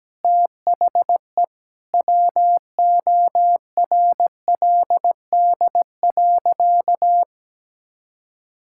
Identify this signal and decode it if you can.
Morse code, heard as a keyed tone, about 17 wpm, 700 Hz. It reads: THE WORLD.
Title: Sentences from Top 1000 Words - Review Part 6 - 17wpm